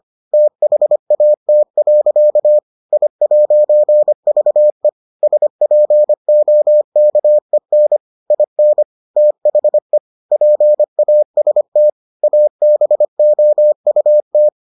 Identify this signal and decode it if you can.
Morse code, heard as a keyed tone, about 25 wpm, 600 Hz. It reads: THAT. I'VE SPOKEN IN THE PAST ABOUT